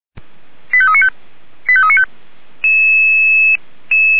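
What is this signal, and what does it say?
Five-tone selective call (ZVEI), heard as quick runs of five tones, tones of about 70 ms. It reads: 85395 85395
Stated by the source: ZVEI five-tone selective call received with an SDR.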